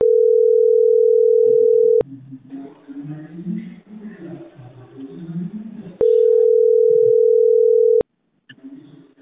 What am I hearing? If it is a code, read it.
Ringback tone (North America), 440 + 480 Hz.